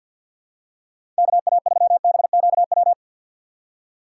Morse code, heard as a keyed tone, about 48 wpm, 700 Hz. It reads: KA3BQW
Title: USA Call Signs - 6 Characters - 6x Character Spacing - 50wpm